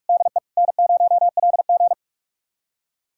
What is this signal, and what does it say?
Morse code, heard as a keyed tone, about 42 wpm, 700 Hz. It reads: DE N0LG
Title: USA Callsign DE Callsign - 4 Characters - 45wpm